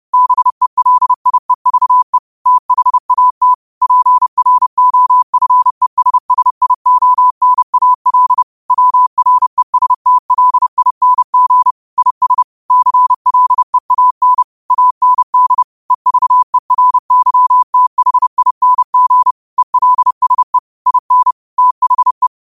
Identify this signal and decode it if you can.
Morse code, heard as a keyed tone, about 30 wpm, 1000 Hz. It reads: BELIEVE THAT PROFESSIONAL WRESTLING IS CLEAN AND EVERYTHING ELSE IN THE
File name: qotdfiles / 30wpm